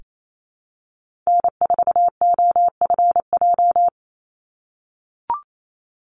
Morse code, heard as a keyed tone, about 28 wpm, 700 Hz. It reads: N4OFJ
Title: USA Call Signs 28wpm